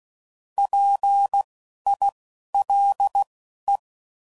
Morse code, heard as a keyed tone, about 16 wpm, spaced as for 12 wpm, 800 Hz. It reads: PILE
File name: Prefixes / By First Letter / Ps